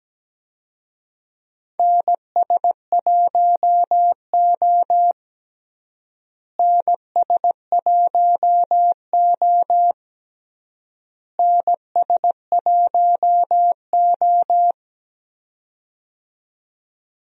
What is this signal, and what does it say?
Morse code, heard as a keyed tone, about 17 wpm, 700 Hz. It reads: NS1O NS1O NS1O
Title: USA Call Signs - 4 Characters - Repeated 3x - 17wpm